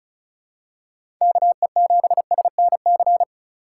Morse code, heard as a keyed tone, about 34 wpm, 700 Hz. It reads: KE7SNC
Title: USA Call Signs - 4x Character Spacing - 35wpm